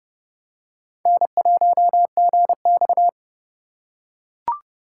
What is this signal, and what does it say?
Morse code, heard as a keyed tone, about 30 wpm, 700 Hz. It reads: N1GX
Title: USA Call Signs - 4 Characters - 2x Character Spacing - 30wpm